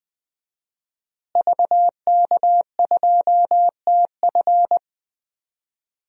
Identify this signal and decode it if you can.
Morse code, heard as a keyed tone, about 20 wpm, 700 Hz. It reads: VK2TF
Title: DXCC-100 Call Signs - 5 Characters - Repeated 3x - 20wpm